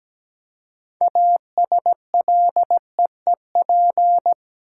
Morse code, heard as a keyed tone, about 17 wpm, 700 Hz. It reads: ASLEEP
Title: Six Letter Words - 17wpm